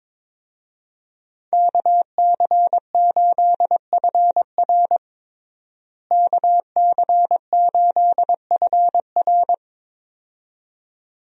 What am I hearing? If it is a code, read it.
Morse code, heard as a keyed tone, about 22 wpm, 700 Hz. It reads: KC8FR KC8FR
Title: USA Call Signs - 5 Characters - Repeated 2x - 22wpm